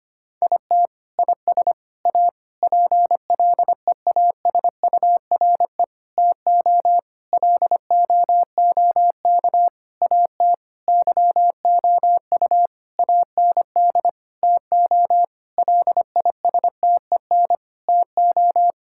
Morse code, heard as a keyed tone, about 25 wpm, 700 Hz. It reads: IT IS A PLEASURE TO LOOK AT YOU AND TO LISTEN TO
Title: Sentences from Top 1600 Words 25wpm